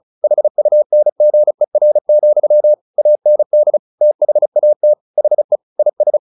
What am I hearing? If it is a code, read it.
Morse code, heard as a keyed tone, about 34 wpm, 600 Hz. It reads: HUNGER, AND THAT HE IS
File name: the old man and the sea 2 35wpm